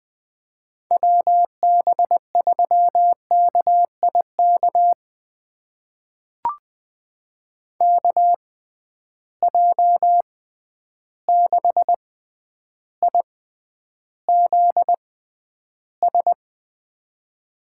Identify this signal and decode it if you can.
Morse code, heard as a keyed tone, about 20 wpm, spaced as for 10 wpm, 700 Hz. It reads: WB3KIK KJ6IZS
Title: USA Call Signs - 6 Characters - 6x Character Spacing - 20wpm